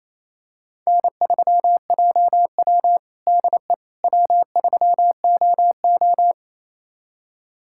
Morse code, heard as a keyed tone, about 28 wpm, 700 Hz. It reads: N3JW DE W3OO
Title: USA Callsign DE Callsign - 4 Characters - 28wpm